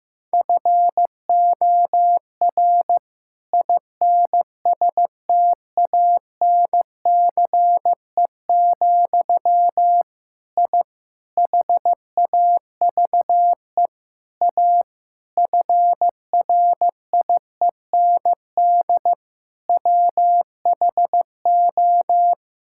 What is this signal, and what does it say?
Morse code, heard as a keyed tone, about 15 wpm, 700 Hz. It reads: FOR INSTANCE, I HAVE A FRIEND WHO
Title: Sentences from Top 1000 Words - Review Part 5 - 15wpm